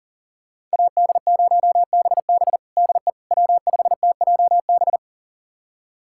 Morse code, heard as a keyed tone, about 40 wpm, 700 Hz. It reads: AD0BB DE W5TJB
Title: USA Callsign DE Callsign - 5 Characters - 40wpm